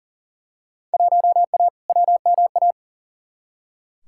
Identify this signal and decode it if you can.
Morse code, heard as a keyed tone, about 40 wpm, 700 Hz. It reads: 1A WMA
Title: Field Day Exchanges - 40wpm